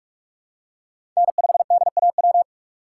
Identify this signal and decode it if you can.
Morse code, heard as a keyed tone, about 45 wpm, 700 Hz. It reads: N5DAW
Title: USA Call Signs - 5 Characters - 3x Character Spacing - 45wpm